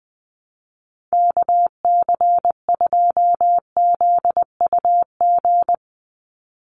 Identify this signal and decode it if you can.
Morse code, heard as a keyed tone, about 20 wpm, 700 Hz. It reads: KC2ZUG
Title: USA Call Signs 20wpm